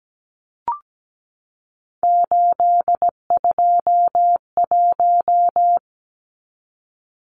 Morse code, heard as a keyed tone, about 17 wpm, 700 Hz. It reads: 821
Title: Numbers - 3-Digit 17wpm